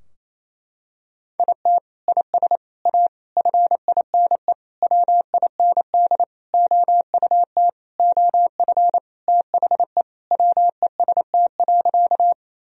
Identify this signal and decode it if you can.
Morse code, heard as a keyed tone, about 28 wpm, 700 Hz. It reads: IT IS A FINE WIND OUT OF THE WEST.
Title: Sentences from Top 1000 Words - Review Part 2 - 28wpm